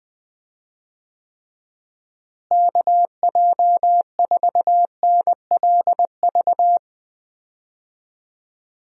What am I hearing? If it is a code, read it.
Morse code, heard as a keyed tone, about 20 wpm, 700 Hz. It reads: KJ4NLV
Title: USA Call Signs - 6 Characters - Repeated 2x - 20wpm